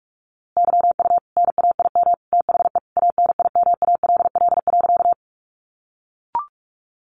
Morse code, heard as a keyed tone, about 44 wpm, 700 Hz. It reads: YU NAIM THE ANIMALL.
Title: A Sound Of Thunder 45wpm